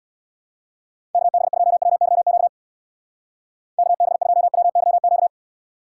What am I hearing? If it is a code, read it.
Morse code, heard as a keyed tone, about 61 wpm, 700 Hz. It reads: KB2WYQ KB2WYQ
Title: Speed Racing - USA 2x3 Call Signs 50wpm